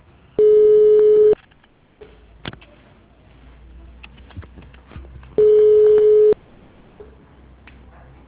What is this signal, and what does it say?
Ringback tone (Europe, 425 Hz).